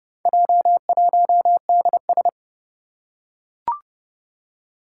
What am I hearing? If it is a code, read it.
Morse code, heard as a keyed tone, about 30 wpm, 700 Hz. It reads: J1DS
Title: Four Challenging Letter-Number Combinations 30wpm